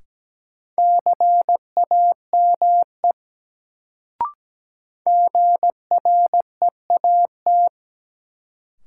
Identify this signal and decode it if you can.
Morse code, heard as a keyed tone, about 17 wpm, 700 Hz. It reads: CAME GREAT
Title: Sentences from Top 200 Words - New Word Drill - 17wpm